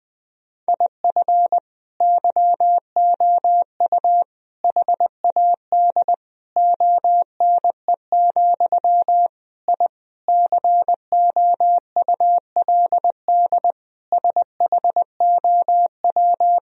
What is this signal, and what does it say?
Morse code, heard as a keyed tone, about 20 wpm, 700 Hz. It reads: IF YOU HAD ONE, I COULD SHOW
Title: Sentences from Top 1000 Words - Review Part 3 - 20wpm